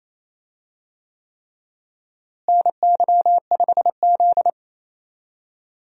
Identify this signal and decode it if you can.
Morse code, heard as a keyed tone, about 28 wpm, 700 Hz. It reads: NY5Z
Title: USA Call Signs - 4 Characters - Repeated 3x - 28wpm